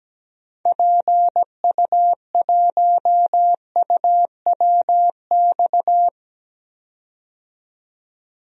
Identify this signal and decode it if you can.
Morse code, heard as a keyed tone, about 17 wpm, 700 Hz. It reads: PU1UWX
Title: DXCC-100 Call Signs - 6 Characters - 17wpm